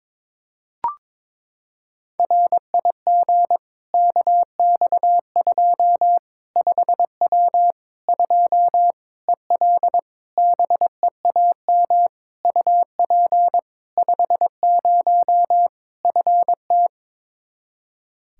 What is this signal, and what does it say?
Morse code, heard as a keyed tone, about 22 wpm, 700 Hz. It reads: RIG KX2 5W 2 EL BEAM UP 50 FT
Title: Practice QSO - Exchange 2 - Info - 22wpm